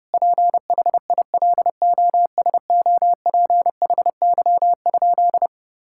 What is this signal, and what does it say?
Morse code, heard as a keyed tone, about 30 wpm, 700 Hz. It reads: PHILOSOPHY?
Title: Sentences from Top 1000 Words - Review Part 6 - 30wpm